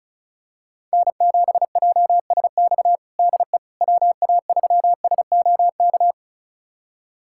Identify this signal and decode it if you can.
Morse code, heard as a keyed tone, about 36 wpm, 690 Hz. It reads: N7JSX DE WA3SOK
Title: USA Callsign DE Callsign 35wpm